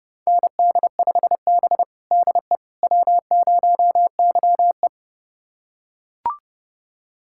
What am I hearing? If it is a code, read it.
Morse code, heard as a keyed tone, about 30 wpm, 700 Hz. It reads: ND5B DE W0YE
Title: USA Callsign DE Callsign - 4 Characters - 30wpm